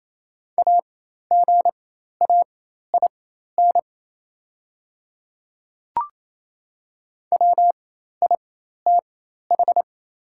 Morse code, heard as a keyed tone, about 28 wpm, spaced as for 13 wpm, 700 Hz. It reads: AGAIN WITH
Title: 4x Character Spacing - Top 300 Words - 28wpm